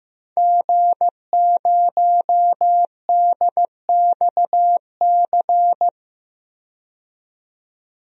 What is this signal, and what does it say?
Morse code, heard as a keyed tone, about 16 wpm, 700 Hz. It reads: G0DXC
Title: DXCC-100 Call Signs - 5 Characters - 15wpm